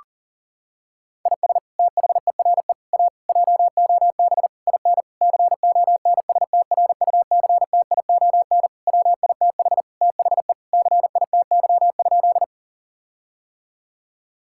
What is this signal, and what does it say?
Morse code, heard as a keyed tone, about 40 wpm, 700 Hz. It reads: IS THERE A JOB IN CONSTRUCTION WITH THE CITY?
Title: Sentences from Top 1600 Words 40wpm